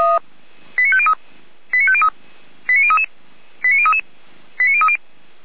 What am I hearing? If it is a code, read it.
Five-tone selective call (ZVEI), heard as quick runs of five tones, tones of about 70 ms. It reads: 89503 89503 89033 89033 89030
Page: /455/SDR/ZVEI/89099/